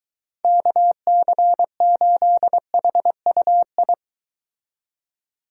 Morse code, heard as a keyed tone, about 23 wpm, 700 Hz. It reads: KC8HUI